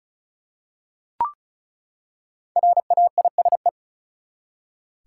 Morse code, heard as a keyed tone, about 36 wpm, 700 Hz. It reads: RAISE
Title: Five Letter Words - 35wpm